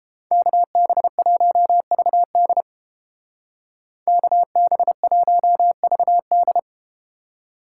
Morse code, heard as a keyed tone, about 31 wpm, 700 Hz. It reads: KB1VD KB1VD